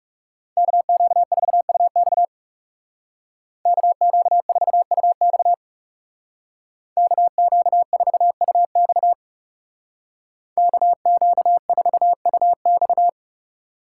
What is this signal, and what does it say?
Morse code, heard as a keyed tone, about 36 wpm, 700 Hz. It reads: KQ4UX KQ4UX KQ4UX KQ4UX
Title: Speed Racing - USA 2x2 Call Signs 30wpm